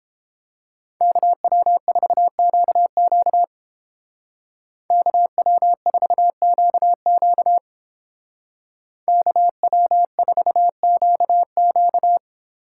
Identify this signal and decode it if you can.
Morse code, heard as a keyed tone, about 30 wpm, 700 Hz. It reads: KW4QQ KW4QQ KW4QQ